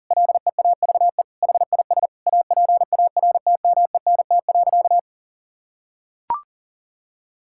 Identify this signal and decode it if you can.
Morse code, heard as a keyed tone, about 40 wpm, 700 Hz. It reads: LEAVE HIS APARTMENT.
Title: Sentences from Top 1000 Words - Review Part 6 - 40wpm